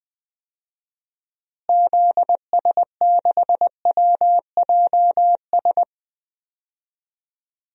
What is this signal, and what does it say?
Morse code, heard as a keyed tone, about 20 wpm, 700 Hz. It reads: ZS6WJS
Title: DXCC-100 Call Signs - 6 Characters - Repeated 3x - 20wpm